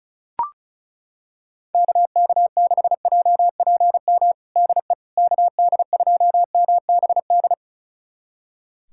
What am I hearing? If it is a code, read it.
Morse code, heard as a keyed tone, about 36 wpm, 700 Hz. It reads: KK6JPM DE KD2MBD
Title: USA Callsign DE Callsign - 6 Characters - 35wpm